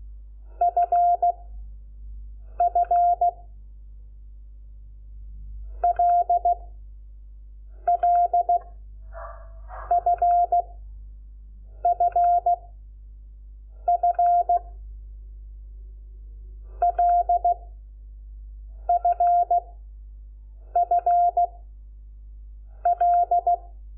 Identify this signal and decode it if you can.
Morse code, heard as a keyed tone, about 16 wpm, 700 Hz. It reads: F F L L F F F L F F L
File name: Lecciones / CLASE Nº5